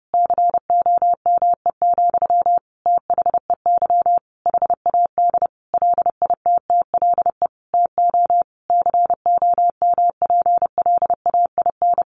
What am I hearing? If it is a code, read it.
Morse code, heard as a keyed tone, about 30 wpm, 700 Hz. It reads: COME, THEY HAD LITTLE TO COMPLAIN
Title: Sentences from Top 1300 Words 30wpm